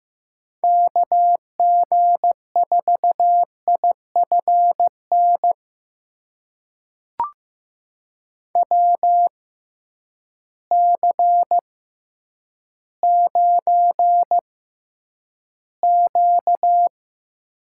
Morse code, heard as a keyed tone, about 15 wpm, spaced as for 9 wpm, 700 Hz. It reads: KG4IFN WC9Q
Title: USA Call Signs - 6x Character Spacing - 15wpm